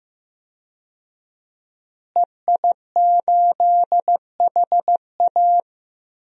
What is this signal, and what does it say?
Morse code, heard as a keyed tone, about 15 wpm, 700 Hz. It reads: EI8HA
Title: DXCC-100 Call Signs - 5 Characters - Repeated 3x - 15wpm